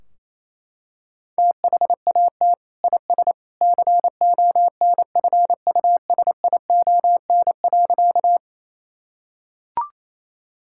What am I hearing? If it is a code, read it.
Morse code, heard as a keyed tone, about 28 wpm, 700 Hz. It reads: THAT IS CONFUSION.
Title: Sentences from Top 1600 Words 28wpm